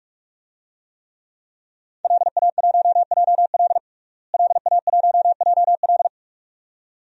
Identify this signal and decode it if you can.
Morse code, heard as a keyed tone, about 46 wpm, 700 Hz. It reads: LA1JL LA1JL
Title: DXCC-100 Call Signs - 5 Characters - Repeated 2x - 45wpm